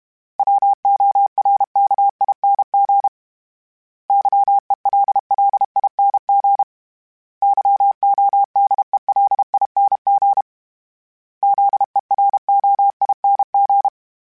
Morse code, heard as a keyed tone, about 32 wpm, 800 Hz. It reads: WORKING YELLING YODELING ZEROING
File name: Special INGs-mp3s / Special INGs-mp3s / Sngl Ing @30